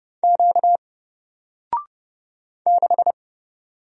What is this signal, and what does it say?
Morse code, heard as a keyed tone, about 30 wpm, 700 Hz. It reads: Q 6
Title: Single Letter-Number 30wpm